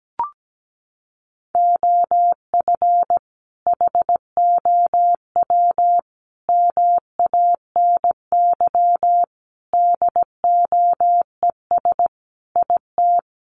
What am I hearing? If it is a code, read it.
Morse code, heard as a keyed tone, about 17 wpm, 700 Hz. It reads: OF HOW MANY DOES IT
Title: Sentences from Top 1400 Words 17wpm